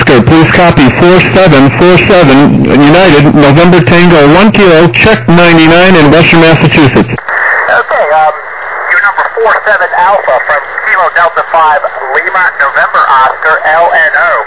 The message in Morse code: none